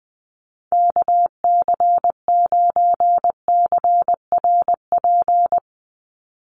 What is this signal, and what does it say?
Morse code, heard as a keyed tone, about 20 wpm, 700 Hz. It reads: KC9CRP
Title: USA Call Signs 20wpm